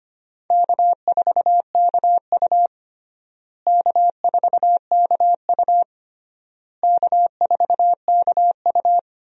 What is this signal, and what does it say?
Morse code, heard as a keyed tone, about 25 wpm, 700 Hz. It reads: K4KU K4KU K4KU
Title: USA Call Signs - 4 Characters - Repeated 3x - 25wpm